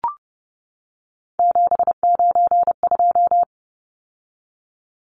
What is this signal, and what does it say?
Morse code, heard as a keyed tone, about 30 wpm, 700 Hz. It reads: 792
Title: Numbers - 3-Digit 30wpm